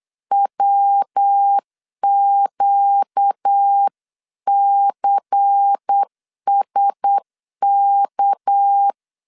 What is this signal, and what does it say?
Morse code, heard as a keyed tone, about 8 wpm, 800 Hz. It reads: WQCSK
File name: CW / DET / 1 6 1500C S B 8wpm 5142